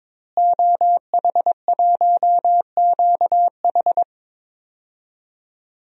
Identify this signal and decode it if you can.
Morse code, heard as a keyed tone, about 21 wpm, 700 Hz. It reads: OH1QH
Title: DXCC-100 Call Signs - 5 Characters - Repeated 3x - 22wpm